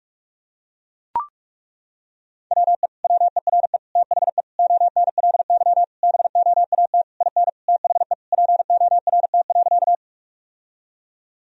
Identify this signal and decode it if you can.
Morse code, heard as a keyed tone, about 41 wpm, 700 Hz. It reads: WE WERE THE ONLY BOAT IN THE PORT.
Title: Sentences from Top 1000 Words - Review Part 1 - 45wpm